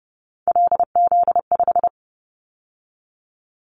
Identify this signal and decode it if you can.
Morse code, heard as a keyed tone, about 30 wpm, 700 Hz. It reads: LZ5
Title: Callsign Prefixes for Top 50 DXCC - 30wpm